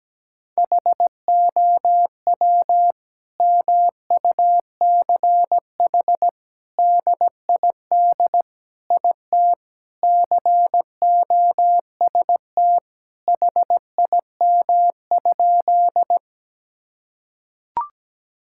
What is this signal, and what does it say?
Morse code, heard as a keyed tone, about 17 wpm, 700 Hz. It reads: HOW MUCH DID IT COST HIM?
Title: Sentences from Top 1000 Words - Review Part 5 - 17wpm